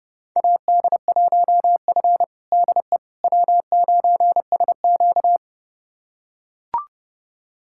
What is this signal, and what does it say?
Morse code, heard as a keyed tone, about 30 wpm, 700 Hz. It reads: AD1F DE W9SQ
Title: USA Callsign DE Callsign - 4 Characters - 30wpm